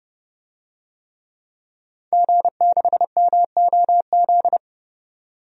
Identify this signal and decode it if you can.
Morse code, heard as a keyed tone, about 29 wpm, 690 Hz. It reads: G6MOZ
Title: DXCC-100 Call Signs - 5 Characters - 30wpm